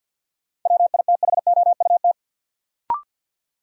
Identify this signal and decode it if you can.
Morse code, heard as a keyed tone, about 48 wpm, 700 Hz. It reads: WITHOUT
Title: Sentences from Top 600 Words - New Word Drill - 50wpm